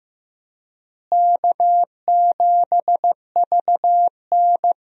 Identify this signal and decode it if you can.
Morse code, heard as a keyed tone, about 15 wpm, 700 Hz. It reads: K7VN
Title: USA Call Signs - 4 Characters - Repeated 3x - 15wpm